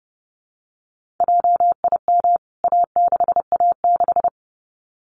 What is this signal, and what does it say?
Morse code, heard as a keyed tone, about 30 wpm, 700 Hz. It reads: JIM A6A6
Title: CWT Exchange 30wpm